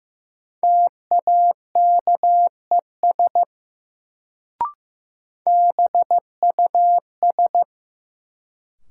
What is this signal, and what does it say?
Morse code, heard as a keyed tone, about 15 wpm, 700 Hz. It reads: TAKES BUS
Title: Sentences from Top 1000 Words - Review Part 6 - New Word Drill - 15wpm